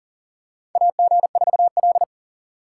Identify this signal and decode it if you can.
Morse code, heard as a keyed tone, about 40 wpm, 700 Hz. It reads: AG4L
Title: USA Call Signs - 4 Characters - 6x Character Spacing - 40wpm